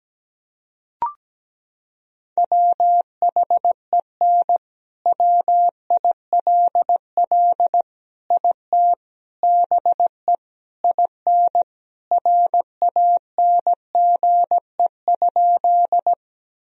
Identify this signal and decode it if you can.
Morse code, heard as a keyed tone, about 17 wpm, 700 Hz. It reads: WHEN WILL IT BE IN RANGE?
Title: Sentences from Top 1000 Words - Review Part 4 - 17wpm